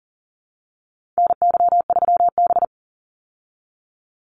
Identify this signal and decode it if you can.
Morse code, heard as a keyed tone, about 40 wpm, 700 Hz. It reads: NY3B
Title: Call Signs Repeated 3x - 40wpm